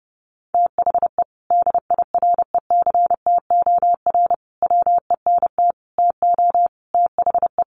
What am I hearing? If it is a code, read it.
Morse code, heard as a keyed tone, about 30 wpm, 700 Hz. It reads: THE DIRECTOR WENT TO THE
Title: Sentences from Top 1300 Words 30wpm